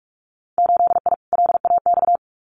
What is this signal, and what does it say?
Morse code, heard as a keyed tone, about 46 wpm, 700 Hz. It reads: 8I LAX
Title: Winter Field Day Exchanges Encore 45wpm